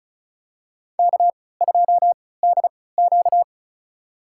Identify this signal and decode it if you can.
Morse code, heard as a keyed tone, about 34 wpm, spaced as for 20 wpm, 700 Hz. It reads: K2DQ
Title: USA Call Signs - 4 Characters - 3x Character Spacing - 35wpm